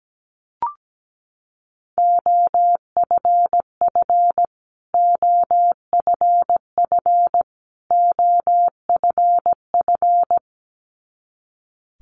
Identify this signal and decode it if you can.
Morse code, heard as a keyed tone, about 17 wpm, 700 Hz. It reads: OFF OFF OFF
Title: Top 500 Words Repeated 17wpm